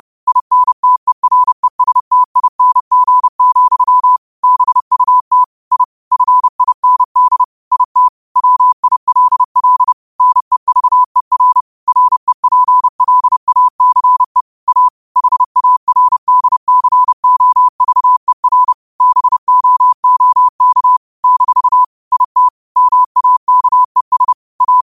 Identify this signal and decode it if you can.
Morse code, heard as a keyed tone, about 30 wpm, 1000 Hz. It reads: INTERESTING, BUT I FIND IT WILL NEVER REPLACE A HARDCOVER BOOK - IT MAKES A
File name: qotdfiles / 30wpm